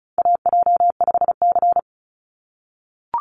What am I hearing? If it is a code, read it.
Morse code, heard as a keyed tone, about 34 wpm, 700 Hz. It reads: AJ5C
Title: NA SOTA Chaser Callsigns 35wpm